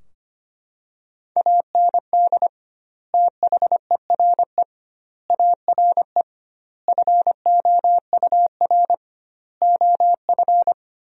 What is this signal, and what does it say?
Morse code, heard as a keyed tone, about 25 wpm, 700 Hz. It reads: AND THERE ARE FOUR OF
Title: 2x Word-Spacing - Sentences from Top 300 Words - 25wpm